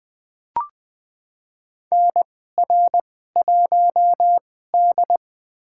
Morse code, heard as a keyed tone, about 21 wpm, spaced as for 15 wpm, 700 Hz. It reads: NR1D
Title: USA Call Signs - 4 Characters - 2x Character Spacing - 20wpm